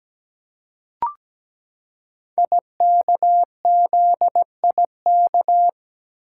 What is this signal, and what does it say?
Morse code, heard as a keyed tone, about 17 wpm, 700 Hz. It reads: IKZIK
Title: Code Groups - 5 Letters - 17wpm